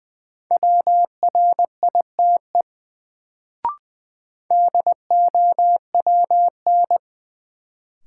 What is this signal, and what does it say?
Morse code, heard as a keyed tone, about 21 wpm, 700 Hz. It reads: WRITE DOWN
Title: Sentences from Top 300 Words - New Word Drill - 20wpm